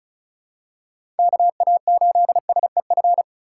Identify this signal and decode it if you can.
Morse code, heard as a keyed tone, about 36 wpm, 700 Hz. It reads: KA8SEF
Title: USA Call Signs - 6 Characters - 2x Character Spacing - 35wpm